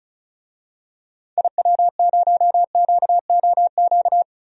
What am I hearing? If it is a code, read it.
Morse code, heard as a keyed tone, about 36 wpm, 700 Hz. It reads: IW0QOQ
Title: DXCC-100 Call Signs - 6 Characters - 35wpm